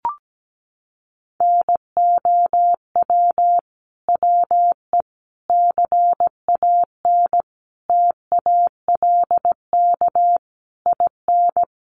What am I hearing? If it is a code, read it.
Morse code, heard as a keyed tone, about 17 wpm, 700 Hz. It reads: NOW WE CAN TALK IN
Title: Sentences from Top 700 Words 17wpm